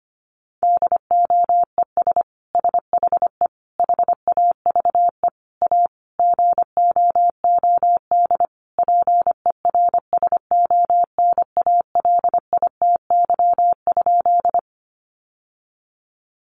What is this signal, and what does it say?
Morse code, heard as a keyed tone, about 25 wpm, 700 Hz. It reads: DOES SHE HAVE A GOOD PERSONALITY?
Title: Sentences from Top 1400 Words 25wpm